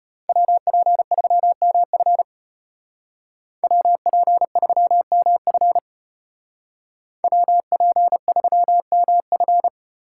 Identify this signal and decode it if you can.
Morse code, heard as a keyed tone, about 33 wpm, 700 Hz. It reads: WP3MF WP3MF WP3MF